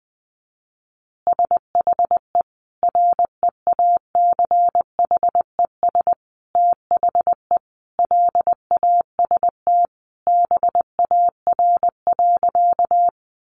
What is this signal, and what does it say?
Morse code, heard as a keyed tone, about 19 wpm, 700 Hz. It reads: SHE REACHES THE LAST BAR.
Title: Easy Phrases 20wpm